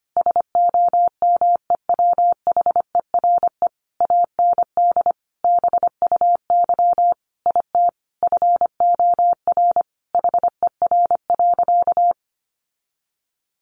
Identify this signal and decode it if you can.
Morse code, heard as a keyed tone, about 25 wpm, 700 Hz. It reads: SOMEWHERE AND BUY IT FOR HER.
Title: Sentences from Top 1400 Words 25wpm